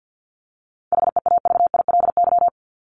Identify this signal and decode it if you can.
Morse code, heard as a keyed tone, about 48 wpm, 700 Hz. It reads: HEAVILY
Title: LY Suffix - Top 100 Words with Base Words 50wpm